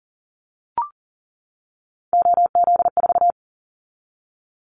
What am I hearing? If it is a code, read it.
Morse code, heard as a keyed tone, about 40 wpm, 700 Hz. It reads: OZ4
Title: Callsign Prefixes for Top 100 DXCC - 40wpm